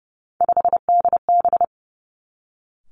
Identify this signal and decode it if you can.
Morse code, heard as a keyed tone, about 30 wpm, 700 Hz. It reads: 5DB
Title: Three Challenging Letter-Number Combinations 30wpm